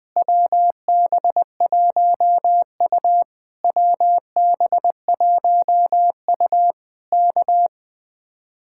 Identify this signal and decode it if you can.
Morse code, heard as a keyed tone, about 20 wpm, 700 Hz. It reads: WB1U WB1U K
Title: Speed-Racing Common QSO Phrases - 20wpm